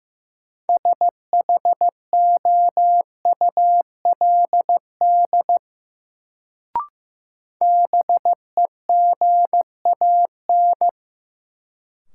Morse code, heard as a keyed tone, about 15 wpm, 700 Hz. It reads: SHOULD BEGAN
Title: Sentences from Top 300 Words - New Word Drill - 15wpm